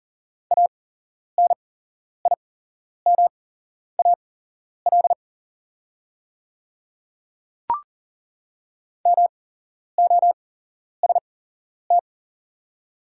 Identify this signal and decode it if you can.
Morse code, heard as a keyed tone, about 40 wpm, spaced as for 11 wpm, 700 Hz. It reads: ANIMAL MOST
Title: 8x Character Spacing - Top 200 Words - 40wpm